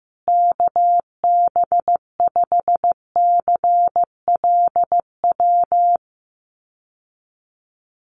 Morse code, heard as a keyed tone, about 15 wpm, 700 Hz. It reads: KB5CLW
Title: USA Call Signs Repeated 2x 15wpm